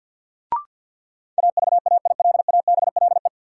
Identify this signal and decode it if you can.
Morse code, heard as a keyed tone, about 51 wpm, 700 Hz. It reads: AVAILABLE
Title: Sentences from Top 600 Words - New Word Drill - 50wpm